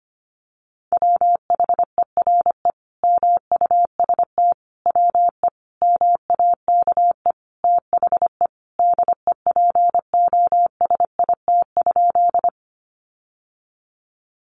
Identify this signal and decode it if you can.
Morse code, heard as a keyed tone, about 25 wpm, 700 Hz. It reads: WHERE MUST WE MAKE THE DEPOSIT?
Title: Sentences from Top 1300 Words 25wpm